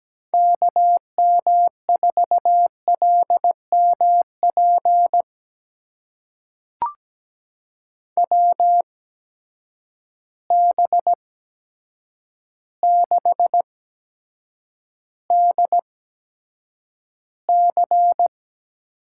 Morse code, heard as a keyed tone, about 17 wpm, spaced as for 8 wpm, 700 Hz. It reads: KM4LMP WB6DC
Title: USA Call Signs - 6 Characters - 8x Character Spacing - 17wpm